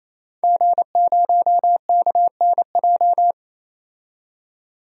Morse code, heard as a keyed tone, about 27 wpm, 700 Hz. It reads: G0KNJ
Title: DXCC-100 Call Signs - 5 Characters - 28wpm